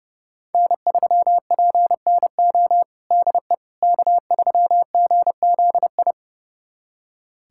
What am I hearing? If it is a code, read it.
Morse code, heard as a keyed tone, about 30 wpm, 700 Hz. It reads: N3PNO DE K3GZI
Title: USA Callsign DE Callsign - 5 Characters - 30wpm